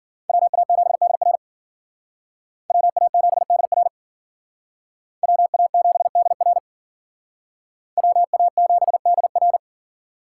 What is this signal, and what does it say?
Morse code, heard as a keyed tone, about 51 wpm, 700 Hz. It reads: WA7DR WA7DR WA7DR WA7DR